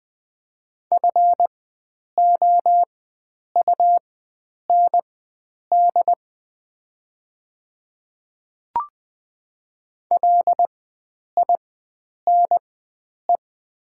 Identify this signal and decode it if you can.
Morse code, heard as a keyed tone, about 20 wpm, spaced as for 9 wpm, 700 Hz. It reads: FOUND LINE
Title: 4x Character Spacing - Top 300 Words - 20wpm